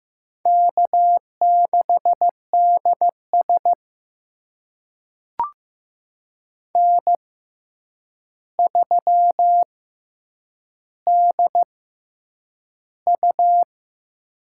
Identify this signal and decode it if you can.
Morse code, heard as a keyed tone, about 15 wpm, spaced as for 8 wpm, 700 Hz. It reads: K6DS N3DU
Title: USA Call Signs - 6x Character Spacing - 15wpm